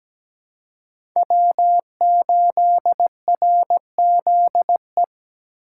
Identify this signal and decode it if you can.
Morse code, heard as a keyed tone, about 17 wpm, 700 Hz. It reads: W8RZE
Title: USA Call Signs - 2x Character Spacing - 17wpm